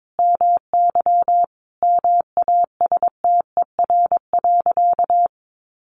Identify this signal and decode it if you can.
Morse code, heard as a keyed tone, about 22 wpm, 700 Hz. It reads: MY MASTER.
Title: Sentences from Top 1300 Words 22wpm